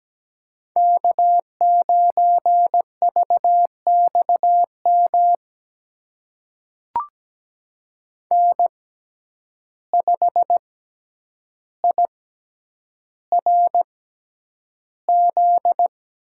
Morse code, heard as a keyed tone, about 17 wpm, spaced as for 9 wpm, 700 Hz. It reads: K9VXM N5IRZ
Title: USA Call Signs - 5 Characters - 6x Character Spacing - 17wpm